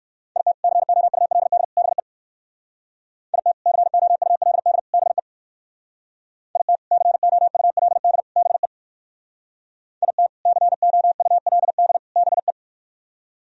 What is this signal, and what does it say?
Morse code, heard as a keyed tone, about 57 wpm, 700 Hz. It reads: IT COULD BE IT COULD BE IT COULD BE IT COULD BE